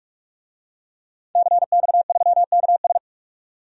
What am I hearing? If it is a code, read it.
Morse code, heard as a keyed tone, about 48 wpm, 700 Hz. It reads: CX3KS
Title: DXCC-100 Call Signs - 5 Characters - 45wpm